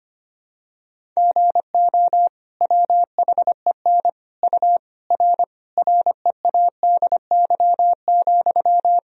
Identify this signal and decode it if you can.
Morse code, heard as a keyed tone, about 25 wpm, 700 Hz. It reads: GO WHEN U R READY,
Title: Sentences from Top 500 Words and Abbreviations 25wpm